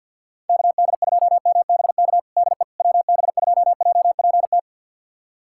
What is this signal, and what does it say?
Morse code, heard as a keyed tone, about 48 wpm, 700 Hz. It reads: KD2MBK DE WB2JPT
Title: USA Callsign DE Callsign 50wpm